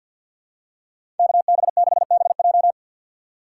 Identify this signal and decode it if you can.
Morse code, heard as a keyed tone, about 50 wpm, 700 Hz. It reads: KB6BJ
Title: USA Call Signs - 5 Characters - 2x Character Spacing - 50wpm